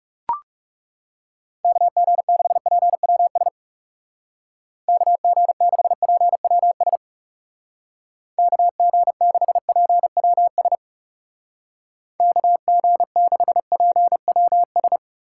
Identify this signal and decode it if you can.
Morse code, heard as a keyed tone, about 37 wpm, 700 Hz. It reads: KG6PWS KG6PWS KG6PWS KG6PWS
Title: Speed Racing - USA 2x3 Call Signs 30wpm